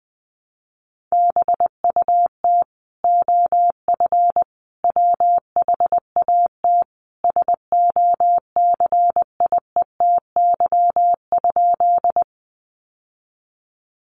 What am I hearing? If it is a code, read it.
Morse code, heard as a keyed tone, about 20 wpm, 700 Hz. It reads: BUT OF WHAT SOCIETY?
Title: Sentences from Top 700 Words 20wpm